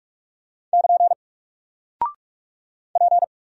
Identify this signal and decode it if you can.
Morse code, heard as a keyed tone, about 44 wpm, 700 Hz. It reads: <KN> P
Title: Single Character (Letter:Number:Punctuation:Prosign) 45wpm